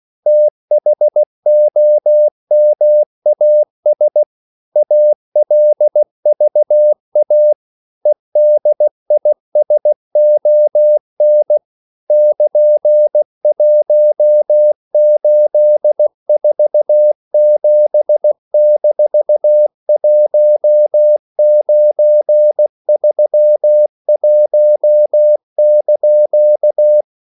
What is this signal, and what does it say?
Morse code, heard as a keyed tone, about 16 wpm, 600 Hz. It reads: THOMAS ALVA EDISON <KN>1847-1931)
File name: CW / Practice / EnglishQuotations0216